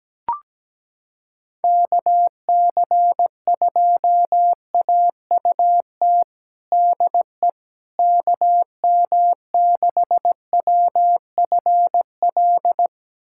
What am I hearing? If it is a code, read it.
Morse code, heard as a keyed tone, about 17 wpm, 700 Hz. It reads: KC2AUT DE KM6WFL
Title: USA Callsign DE Callsign - 6 Characters - 17wpm